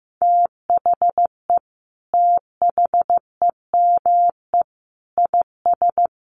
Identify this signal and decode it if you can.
Morse code, heard as a keyed tone, about 15 wpm, 700 Hz. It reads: THE THEME IS
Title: Sentences from Top 1300 Words 15wpm